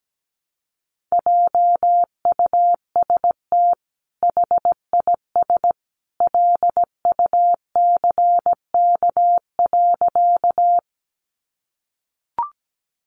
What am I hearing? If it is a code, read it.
Morse code, heard as a keyed tone, about 17 wpm, 700 Hz. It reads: JUST HIS LUCK.
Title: Sentences from Top 1500 Words 17wpm